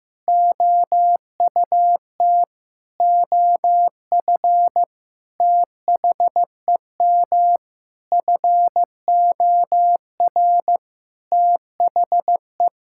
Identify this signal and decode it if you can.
Morse code, heard as a keyed tone, about 16 wpm, 700 Hz. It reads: OUT OF THEM FOR THE
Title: Sentences from Top 1000 Words - Review Part 5 - 15wpm